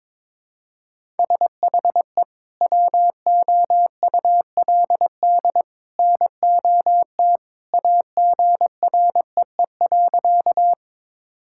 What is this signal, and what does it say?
Morse code, heard as a keyed tone, about 22 wpm, 700 Hz. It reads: SHE WOULD NOT AGREE.
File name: practice / 22wpm / Sentences with 2-Word Phrases - 22wpm